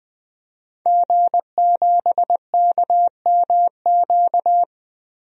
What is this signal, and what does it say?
Morse code, heard as a keyed tone, about 21 wpm, 700 Hz. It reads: G7KMQ
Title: DXCC-100 Call Signs - 5 Characters - Repeated 2x - 20wpm